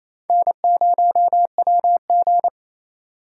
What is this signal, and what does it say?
Morse code, heard as a keyed tone, about 35 wpm, 700 Hz. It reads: N0WG